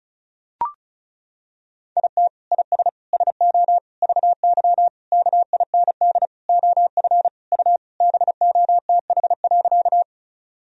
Morse code, heard as a keyed tone, about 34 wpm, 700 Hz. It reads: IT IS SO VY KIND OF U BOTH.
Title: Sentences from Top 300 Words and Abbreviations 35wpm